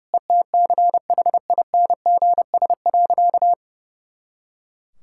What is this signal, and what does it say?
Morse code, heard as a keyed tone, about 32 wpm, 700 Hz. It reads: ETCHINGS.
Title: Pangram Challenge v2 - 30wpm